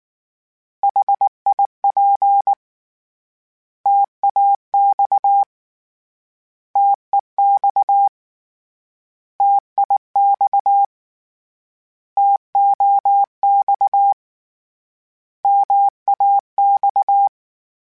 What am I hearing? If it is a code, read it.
Morse code, heard as a keyed tone, about 19 wpm, 800 Hz. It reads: HIP TAX TEX TIX TOX MAX
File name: Contrasts / Contrasts-mp3s / 3LtrContrasts @18